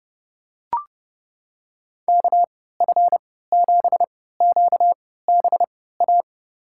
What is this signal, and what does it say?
Morse code, heard as a keyed tone, about 31 wpm, spaced as for 17 wpm, 700 Hz. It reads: KF7QBA
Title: USA Call Signs - 6 Characters - 3x Character Spacing - 30wpm